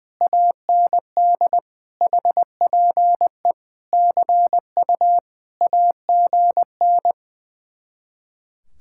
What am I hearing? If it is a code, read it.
Morse code, heard as a keyed tone, about 20 wpm, 700 Hz. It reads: AND HPE CU AGN
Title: Speed-Racing Common QSO Phrases Encore - 20wpm